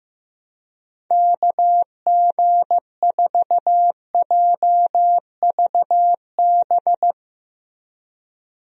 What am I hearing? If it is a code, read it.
Morse code, heard as a keyed tone, about 15 wpm, 700 Hz. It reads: KG4JVB
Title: USA Call Signs - 6 Characters - Repeated 2x - 15wpm